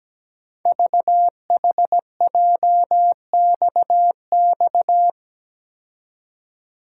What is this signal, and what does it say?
Morse code, heard as a keyed tone, about 17 wpm, 700 Hz. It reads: VHJXX
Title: Code Groups - 5 Letters - 17wpm